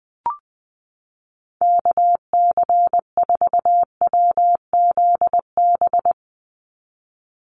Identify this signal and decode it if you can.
Morse code, heard as a keyed tone, about 20 wpm, 700 Hz. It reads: KC4WZB
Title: NA SOTA Activator Callsigns 20wpm